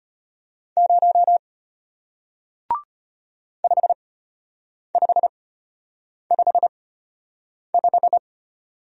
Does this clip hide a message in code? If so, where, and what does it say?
Yes: Morse code from 0.8 to 8.2 s, heard as a keyed tone, about 34 wpm, spaced as for 8 wpm, 700 Hz. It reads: 0 5555